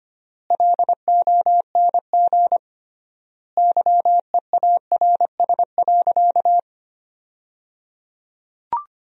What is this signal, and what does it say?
Morse code, heard as a keyed tone, about 25 wpm, 700 Hz. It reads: LONG YEARS.
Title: 3x Word-Spacing - Sentences from Top 200 Words - 25wpm